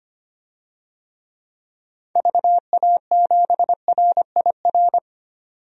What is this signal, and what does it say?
Morse code, heard as a keyed tone, about 25 wpm, 700 Hz. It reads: VA7RIR
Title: DXCC-100 Call Signs - 6 Characters - 25wpm